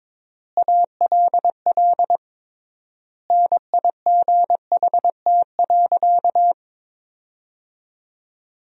Morse code, heard as a keyed tone, about 22 wpm, 700 Hz. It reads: ALL NIGHT.
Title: 3x Word-Spacing - Sentences from Top 300 Words - 22wpm